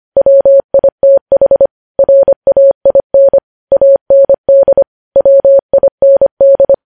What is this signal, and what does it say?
Morse code, heard as a keyed tone, about 25 wpm, 550 Hz. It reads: WITH RAIN AND WIND
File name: podcast / News headlines-1x-25x25